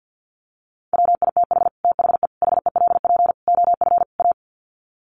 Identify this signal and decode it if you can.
Morse code, heard as a keyed tone, about 48 wpm, 700 Hz. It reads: WITH THE HELP OF A
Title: Sets of 5 Words 50wpm